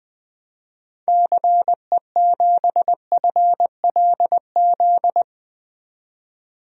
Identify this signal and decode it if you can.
Morse code, heard as a keyed tone, about 20 wpm, 700 Hz. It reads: CE7FLZ
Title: DXCC-100 Call Signs - 6 Characters - 20wpm